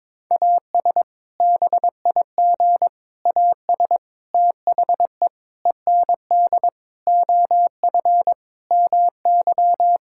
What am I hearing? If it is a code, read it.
Morse code, heard as a keyed tone, about 22 wpm, 700 Hz. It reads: AS BIG AS THE END OF MY
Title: Sentences from Top 1600 Words 22wpm